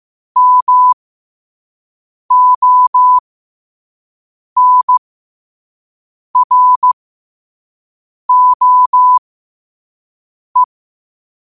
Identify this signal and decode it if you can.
Morse code, heard as a keyed tone, about 15 wpm, spaced as for 5 wpm, 1000 Hz. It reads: MONROE
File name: qotdfiles / 5wpm